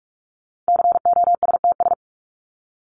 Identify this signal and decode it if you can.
Morse code, heard as a keyed tone, about 44 wpm, 700 Hz. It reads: COSTS
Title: ES:S Suffix - Top 100 Words with Base Words 45wpm